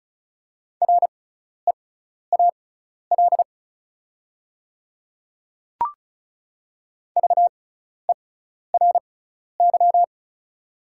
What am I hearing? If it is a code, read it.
Morse code, heard as a keyed tone, about 34 wpm, spaced as for 12 wpm, 700 Hz. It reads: REAL VERY
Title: 6x Character Spacing - Top 300 Words - 35wpm